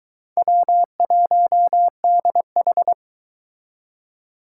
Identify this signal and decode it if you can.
Morse code, heard as a keyed tone, about 23 wpm, 700 Hz. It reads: W1DH